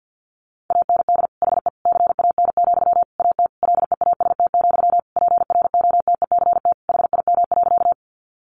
Morse code, heard as a keyed tone, about 50 wpm, 700 Hz. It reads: AND HE CAN, AT LEAST, PROTECT HIM.
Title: Sentences from Top 900 Words 50wpm